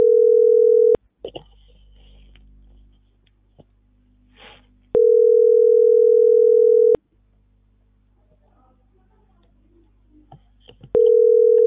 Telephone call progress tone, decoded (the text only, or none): ringback tone